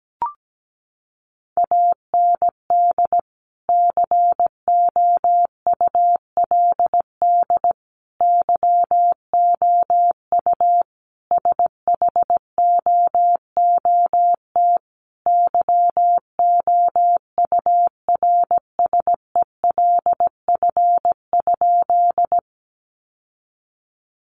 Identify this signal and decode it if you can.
Morse code, heard as a keyed tone, about 17 wpm, 700 Hz. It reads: AND COULD YOU SHOOT YOURSELF?
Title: Sentences from Top 1200 Words 17wpm